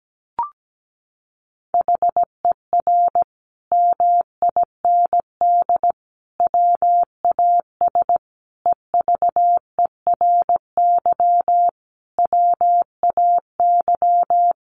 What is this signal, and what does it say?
Morse code, heard as a keyed tone, about 17 wpm, 700 Hz. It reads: HER MIND WAS EVERY WAY
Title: Sentences from Top 1400 Words 17wpm